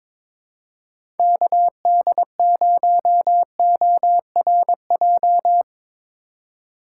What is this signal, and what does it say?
Morse code, heard as a keyed tone, about 22 wpm, 700 Hz. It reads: KD0ORJ
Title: USA Call Signs - 3x Character Spacing - 22wpm